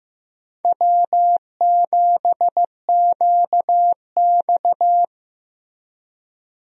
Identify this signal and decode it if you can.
Morse code, heard as a keyed tone, about 15 wpm, 700 Hz. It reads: W7QX
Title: USA Call Signs - 4 Characters - Repeated 2x - 15wpm